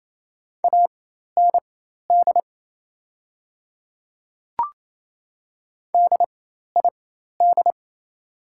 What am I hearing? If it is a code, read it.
Morse code, heard as a keyed tone, about 28 wpm, spaced as for 13 wpm, 700 Hz. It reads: AND DID
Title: 4x Character Spacing - Top 100 Words - 28wpm